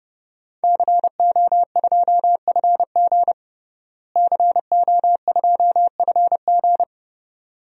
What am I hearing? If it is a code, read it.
Morse code, heard as a keyed tone, about 30 wpm, 700 Hz. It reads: CO2FG CO2FG
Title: DXCC-100 Call Signs - 5 Characters - Repeated 2x - 30wpm